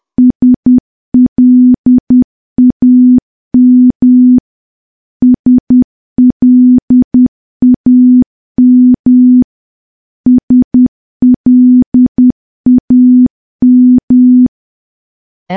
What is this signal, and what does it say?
Morse code, heard as a keyed tone, about 10 wpm, 250 Hz. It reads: SLAM SLAM SLAM
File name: morse / morse-training-10WPM